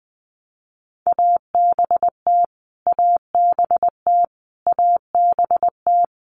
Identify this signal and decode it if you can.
Morse code, heard as a keyed tone, about 20 wpm, 700 Hz. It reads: ABT ABT ABT
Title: Abbreviations Repeated 3x - 20wpm